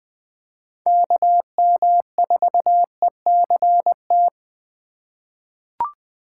Morse code, heard as a keyed tone, about 20 wpm, 700 Hz. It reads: KM4ECT
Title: USA Call Signs - 3x Character Spacing - 20wpm